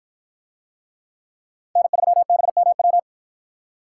Morse code, heard as a keyed tone, about 51 wpm, 700 Hz. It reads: N3BMW